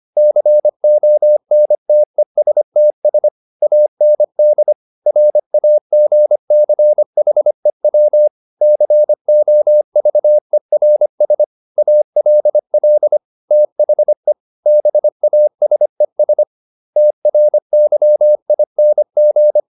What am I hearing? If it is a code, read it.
Morse code, heard as a keyed tone, about 25 wpm, 600 Hz. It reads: CONTESTS AND RAGCHEW COVERS ALL THE BASES TRYING